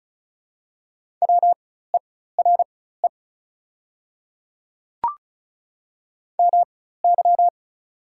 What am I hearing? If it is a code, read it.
Morse code, heard as a keyed tone, about 36 wpm, spaced as for 16 wpm, 700 Hz. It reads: WERE MY
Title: 4x Character Spacing - Top 100 Words - 35wpm